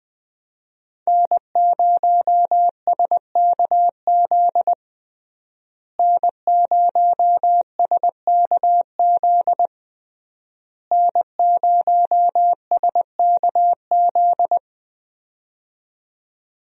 Morse code, heard as a keyed tone, about 20 wpm, 700 Hz. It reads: N0SKZ N0SKZ N0SKZ
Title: USA Call Signs - 5 Characters - Repeated 3x - 20wpm